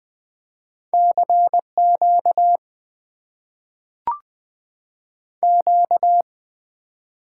Morse code, heard as a keyed tone, about 20 wpm, 700 Hz. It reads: CQ Q
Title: Top 90 QSO Elements - 6x Character Spacing - 20wpm